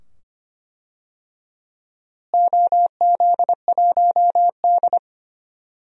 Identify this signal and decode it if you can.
Morse code, heard as a keyed tone, about 25 wpm, 700 Hz. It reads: OZ1D
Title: DXCC-100 Call Signs - 4 Characters - Repeated 2x - 25wpm